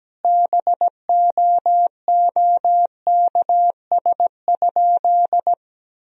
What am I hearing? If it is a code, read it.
Morse code, heard as a keyed tone, about 17 wpm, 700 Hz. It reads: BOOKS?
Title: Sentences from Top 1000 Words - Review Part 4 - 17wpm